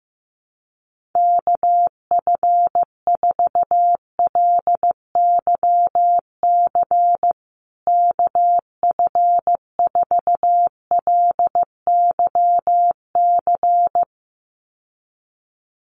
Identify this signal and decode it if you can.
Morse code, heard as a keyed tone, about 14 wpm, 700 Hz. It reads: KF4LYC KF4LYC
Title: USA Call Signs Repeated 2x 15wpm